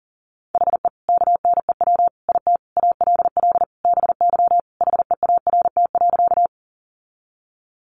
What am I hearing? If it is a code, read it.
Morse code, heard as a keyed tone, about 37 wpm, 700 Hz. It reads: HE KNEW IT ALL BY HEART.
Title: Sentences from Top 1200 Words 40wpm